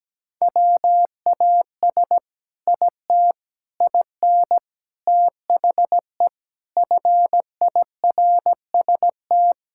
Morse code, heard as a keyed tone, about 17 wpm, 700 Hz. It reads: WAS IT IN THE FIRST
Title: Sentences from Top 1000 Words - Review Part 5 - 17wpm